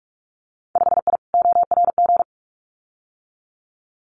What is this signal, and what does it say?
Morse code, heard as a keyed tone, about 46 wpm, 700 Hz. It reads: 5I ORG
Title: Winter Field Day Exchanges Encore 45wpm